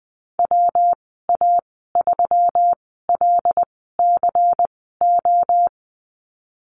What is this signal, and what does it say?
Morse code, heard as a keyed tone, about 19 wpm, spaced as for 14 wpm, 700 Hz. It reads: WA3LCO
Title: 200 Lesson 32g - Letter k - Call Signs 20wpm